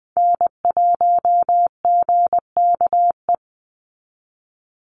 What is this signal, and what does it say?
Morse code, heard as a keyed tone, about 20 wpm, 700 Hz. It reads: N1GKE
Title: Call Signs Repeated 3x - 20wpm